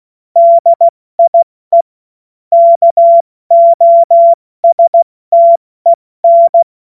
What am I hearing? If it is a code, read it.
Morse code, heard as a keyed tone, about 16 wpm, 680 Hz. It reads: DIE KOSTEN
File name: Morsekurs / Witze / Witz 04. Oktober 2022 70 F80